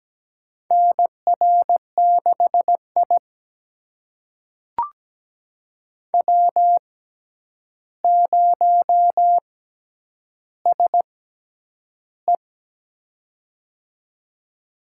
Morse code, heard as a keyed tone, about 17 wpm, spaced as for 9 wpm, 700 Hz. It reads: NR6I W0SE
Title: USA Call Signs - 4 Characters - 6x Character Spacing - 17wpm